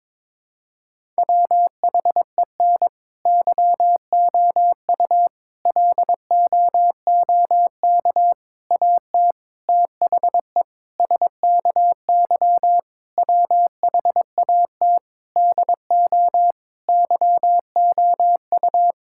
Morse code, heard as a keyed tone, about 22 wpm, 700 Hz. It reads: WHEN YOU LOOK AT THE SKY WHAT DO YOU
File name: practice / 22wpm / Sentences with 3-Word Phrases - 22wpm